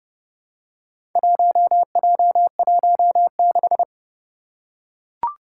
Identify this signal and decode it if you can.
Morse code, heard as a keyed tone, about 30 wpm, 700 Hz. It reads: 1J16
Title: Four Challenging Letter-Number Combinations 30wpm